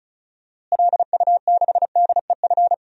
Morse code, heard as a keyed tone, about 36 wpm, 700 Hz. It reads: LU6DEF
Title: DXCC-100 Call Signs - 6 Characters - Repeated 2x - 35wpm